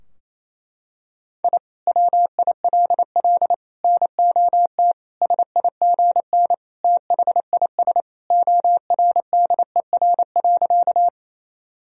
Morse code, heard as a keyed tone, about 28 wpm, 700 Hz. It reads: I WILL NOT SIGN THIS ORDER.
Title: Sentences from Top 1000 Words - Review Part 2 - 28wpm